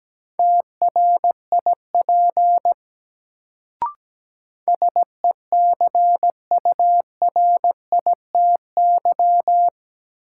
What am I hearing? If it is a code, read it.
Morse code, heard as a keyed tone, about 17 wpm, 700 Hz. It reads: TRIP SECURITY
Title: Sentences from Top 1000 Words - Review Part 4 - New Word Drill - 17wpm